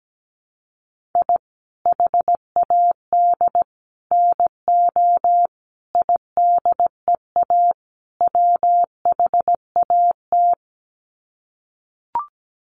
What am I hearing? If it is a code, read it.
Morse code, heard as a keyed tone, about 17 wpm, 700 Hz. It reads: I HAD NO IDEA WHAT
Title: Sets of 5 Words Encore 17wpm